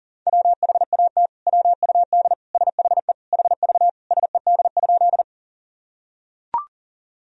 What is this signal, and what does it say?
Morse code, heard as a keyed tone, about 40 wpm, 700 Hz. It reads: WHAT WUD SHE HV SED?
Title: Sentences from Top 100 Words and Abbreviations 40wpm